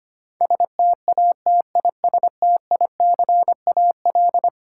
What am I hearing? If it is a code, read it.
Morse code, heard as a keyed tone, about 25 wpm, 700 Hz. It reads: STATISTICAL
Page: morse-code-ninja-practice-25wpm